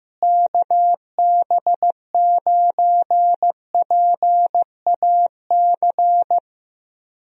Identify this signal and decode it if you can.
Morse code, heard as a keyed tone, about 15 wpm, 700 Hz. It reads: KB9PAC
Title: USA Call Signs - 6 Characters - Repeated 2x - 15wpm